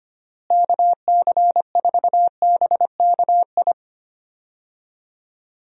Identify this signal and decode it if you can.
Morse code, heard as a keyed tone, about 25 wpm, 700 Hz. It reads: KC4BKI
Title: USA Call Signs - 6 Characters - Repeated 3x - 25wpm